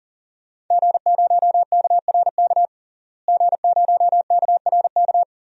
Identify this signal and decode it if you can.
Morse code, heard as a keyed tone, about 40 wpm, 700 Hz. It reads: G0KRK G0KRK
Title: DXCC-100 Call Signs - 5 Characters - Repeated 3x - 40wpm